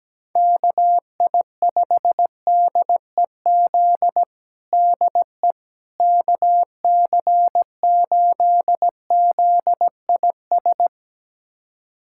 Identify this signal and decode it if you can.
Morse code, heard as a keyed tone, about 17 wpm, 700 Hz. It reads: KI5DEZ DE KC8ZIS
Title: USA Callsign DE Callsign 17wpm